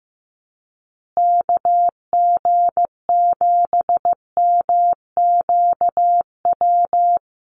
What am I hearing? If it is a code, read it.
Morse code, heard as a keyed tone, about 15 wpm, 700 Hz. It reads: KG7MQW
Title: USA Call Signs Repeated 2x 15wpm